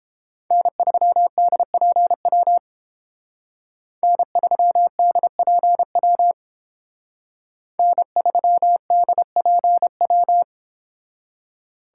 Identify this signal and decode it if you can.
Morse code, heard as a keyed tone, about 29 wpm, 700 Hz. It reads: N3DPW N3DPW N3DPW